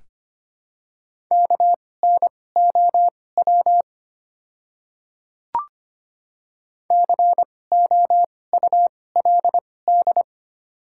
Morse code, heard as a keyed tone, about 25 wpm, spaced as for 18 wpm, 690 Hz. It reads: KNOW COULD
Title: 2x Character Spacing - Top 100 Words - 25wpm